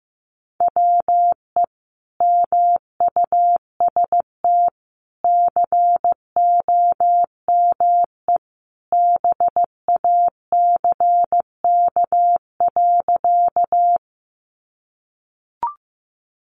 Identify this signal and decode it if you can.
Morse code, heard as a keyed tone, about 15 wpm, 700 Hz. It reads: WE MUST COME BACK.
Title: Sentences from Top 200 Words 15wpm